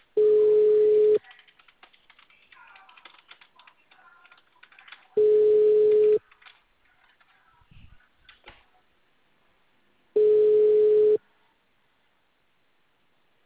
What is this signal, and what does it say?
Ringback tone (Europe, 425 Hz).